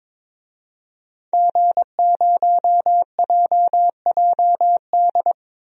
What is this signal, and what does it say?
Morse code, heard as a keyed tone, about 23 wpm, 700 Hz. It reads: G0JJD
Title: DXCC-100 Call Signs - 5 Characters - 22wpm